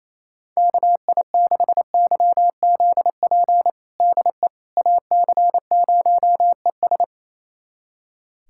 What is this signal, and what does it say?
Morse code, heard as a keyed tone, about 28 wpm, 700 Hz. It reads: KI6YZP DE AC0ES
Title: USA Callsign DE Callsign 28wpm